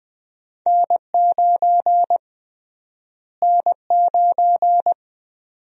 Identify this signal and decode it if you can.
Morse code, heard as a keyed tone, about 20 wpm, 700 Hz. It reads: N9 N9
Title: USA Call Sign Prefixes - Repeated 2x - 20wpm